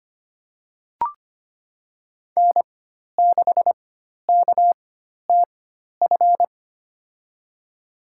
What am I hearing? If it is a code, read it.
Morse code, heard as a keyed tone, about 25 wpm, spaced as for 12 wpm, 700 Hz. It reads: N6KTF
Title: USA Call Signs - 5 Characters - 4x Character Spacing - 25wpm